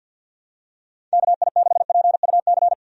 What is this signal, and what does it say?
Morse code, heard as a keyed tone, about 49 wpm, 700 Hz. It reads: KI6PUC
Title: USA Call Signs - 6 Characters - 50wpm